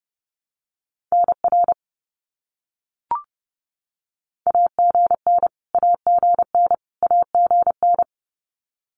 Morse code, heard as a keyed tone, about 30 wpm, 700 Hz. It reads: NR AGN AGN AGN
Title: Abbreviations Repeated 3x - 30wpm